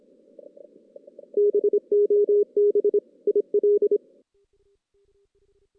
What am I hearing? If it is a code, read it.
Morse code, heard as a keyed tone, about 25 wpm, 410 Hz. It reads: BOB IL
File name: 2016 JAN CW NAQP / 2016 Jan naqp k5ot - 0-5